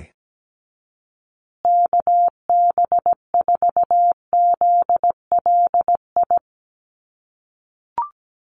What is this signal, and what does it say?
Morse code, heard as a keyed tone, about 17 wpm, 700 Hz. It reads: KB4ZLI
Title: USA Call Signs Repeated 3x 17wpm